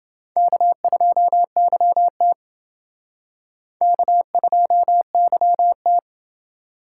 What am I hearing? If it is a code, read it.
Morse code, heard as a keyed tone, about 30 wpm, 700 Hz. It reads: K2YT K2YT